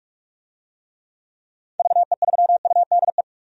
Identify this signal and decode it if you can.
Morse code, heard as a keyed tone, about 47 wpm, 700 Hz. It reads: VE3UDE
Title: DXCC-100 Call Signs - 6 Characters - 45wpm